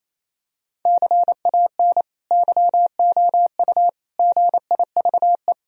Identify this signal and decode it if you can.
Morse code, heard as a keyed tone, about 28 wpm, 700 Hz. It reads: CAN YOU GIVE
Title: Sentences from Top 1000 Words - Review Part 4 - 28wpm